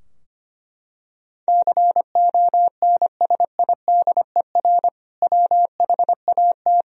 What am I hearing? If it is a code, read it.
Morse code, heard as a keyed tone, about 25 wpm, 700 Hz. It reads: CONSIDER WHAT
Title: Sentences from Top 1000 Words - Review Part 4 - 25wpm